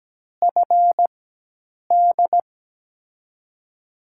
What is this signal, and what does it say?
Morse code, heard as a keyed tone, about 17 wpm, spaced as for 8 wpm, 700 Hz. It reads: FD
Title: USA Call Signs - 5 Characters - 4x Character Spacing - 17wpm